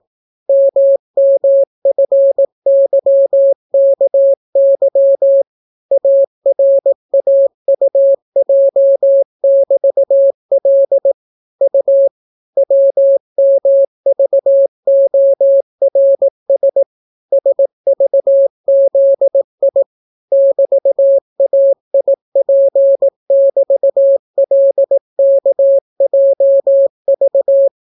Morse code, heard as a keyed tone, about 18 wpm, 550 Hz. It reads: MMFYKY ARAUJ<BT>L U WMVORS SVZI <BT>AIP<BT>LKJV